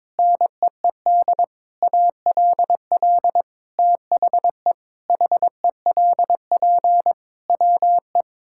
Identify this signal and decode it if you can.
Morse code, heard as a keyed tone, about 22 wpm, 700 Hz. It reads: NEED ALL THE HELP WE
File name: practice / 22wpm / Sentences with 8-Word Phrases - 22wpm